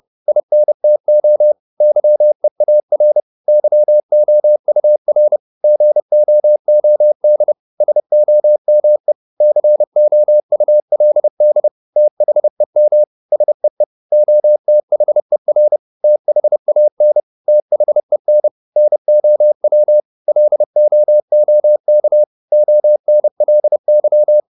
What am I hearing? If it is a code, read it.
Morse code, heard as a keyed tone, about 30 wpm, 600 Hz. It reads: INTO YEAR YOUR GOOD SOME COULD THEM SEE OTHER THAN THEN NOW LOOK ONLY
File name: Top-100-Words-30